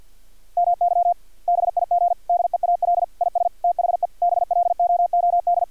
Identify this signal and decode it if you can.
Morse code, heard as a keyed tone, about 50 wpm, 700 Hz. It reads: MY BIG DEAL IS THE BLOOD
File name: archived / CRSnet